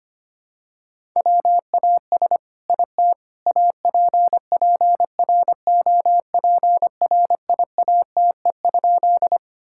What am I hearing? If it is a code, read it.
Morse code, heard as a keyed tone, about 25 wpm, 700 Hz. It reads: WAS IT APPROPRIATE?
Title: Sentences from Top 1000 Words - Review Part 5 - 25wpm